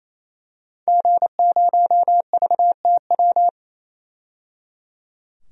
Morse code, heard as a keyed tone, about 28 wpm, 700 Hz. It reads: G0VTW
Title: DXCC-100 Call Signs - 5 Characters - 28wpm